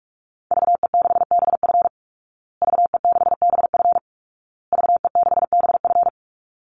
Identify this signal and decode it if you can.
Morse code, heard as a keyed tone, about 44 wpm, 700 Hz. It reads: VE6BF VE6BF VE6BF
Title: Call Signs Repeated 2x - 45wpm